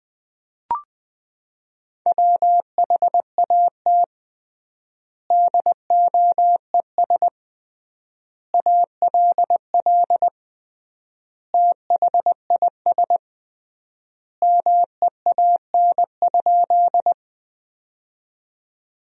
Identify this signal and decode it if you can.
Morse code, heard as a keyed tone, about 21 wpm, 700 Hz. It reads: WHAT DOES ALL THIS MEAN?
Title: 3x Word-Spacing - Sentences from Top 200 Words - 20wpm